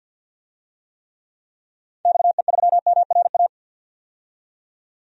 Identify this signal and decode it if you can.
Morse code, heard as a keyed tone, about 52 wpm, 700 Hz. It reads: XE3MRA
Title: DXCC-100 Call Signs - 6 Characters - 50wpm